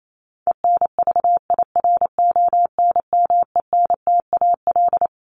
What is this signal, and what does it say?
Morse code, heard as a keyed tone, about 28 wpm, 700 Hz. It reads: ENVIRONMENTAL
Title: Top 1001-1250 Words 28wpm